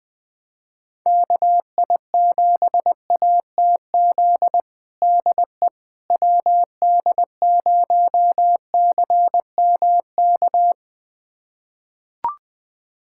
Morse code, heard as a keyed tone, about 20 wpm, 700 Hz. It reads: KI7ATZ DE WD0CMK
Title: USA Callsign DE Callsign - 6 Characters - 20wpm